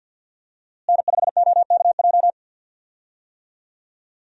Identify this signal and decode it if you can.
Morse code, heard as a keyed tone, about 47 wpm, 700 Hz. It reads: N5OKJ
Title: USA Call Signs - 5 Characters - Repeated 2x - 50wpm